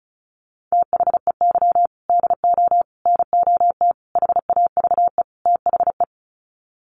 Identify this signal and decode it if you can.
Morse code, heard as a keyed tone, about 34 wpm, 700 Hz. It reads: THEY DO NOT HAVE THE
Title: Sets of 5 Words 35wpm